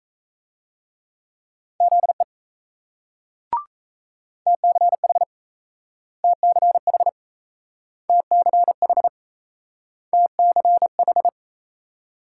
Morse code, heard as a keyed tone, about 35 wpm, 700 Hz. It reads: ZE TCH TCH TCH TCH